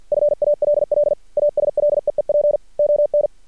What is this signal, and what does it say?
Morse code, heard as a keyed tone, about 50 wpm, 590 Hz. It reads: FALL ASLEEP ON
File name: archived / CRSnet